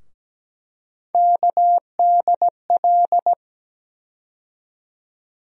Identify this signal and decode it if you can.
Morse code, heard as a keyed tone, about 17 wpm, 700 Hz. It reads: KDL
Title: USA Call Sign Postfixes - 3 Letters 17wpm